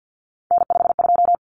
Morse code, heard as a keyed tone, about 50 wpm, 700 Hz. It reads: N52
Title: Common Cut Numbers - 3-Digit 50wpm